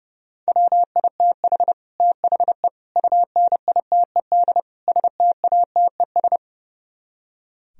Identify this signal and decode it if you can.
Morse code, heard as a keyed tone, about 38 wpm, 700 Hz. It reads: WITH THE UNITED STATES